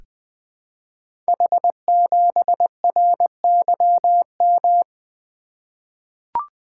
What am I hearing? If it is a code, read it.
Morse code, heard as a keyed tone, about 20 wpm, 700 Hz. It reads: H7RYM
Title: Code Groups - 5 Letter:Numbers - 20wpm